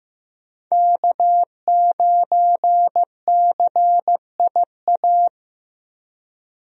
Morse code, heard as a keyed tone, about 15 wpm, 700 Hz. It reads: K9CIA
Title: USA Call Signs - 5 Characters - Repeated 3x - 15wpm